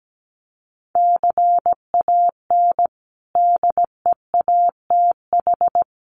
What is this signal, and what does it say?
Morse code, heard as a keyed tone, about 17 wpm, 700 Hz. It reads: CAN DEATH
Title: Sentences from Top 1200 Words 17wpm